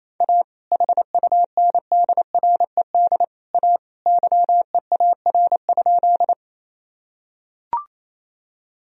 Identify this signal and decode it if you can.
Morse code, heard as a keyed tone, about 28 wpm, 700 Hz. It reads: A HUNDRED A YEAR?
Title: Sentences from Top 1000 Words - Review Part 2 - 28wpm